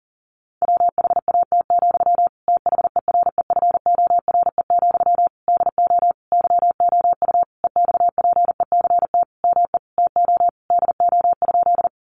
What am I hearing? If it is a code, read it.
Morse code, heard as a keyed tone, about 40 wpm, 700 Hz. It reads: WHAT, THEREFORE, DO YOU EXPECT ME TO DO?
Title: Sentences from Top 1100 Words 40wpm